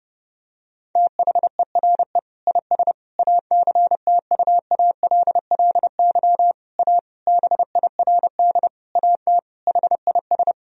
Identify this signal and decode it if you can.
Morse code, heard as a keyed tone, about 30 wpm, 700 Hz. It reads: THERE IS ACTUALLY A BIRD AT HIS
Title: Sentences from Top 1000 Words - Review Part 4 - 30wpm